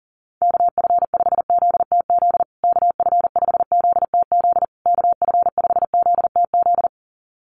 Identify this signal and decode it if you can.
Morse code, heard as a keyed tone, about 37 wpm, 700 Hz. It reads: KF5ZTZ KF5ZTZ KF5ZTZ
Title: USA Call Signs Repeated 3x 40wpm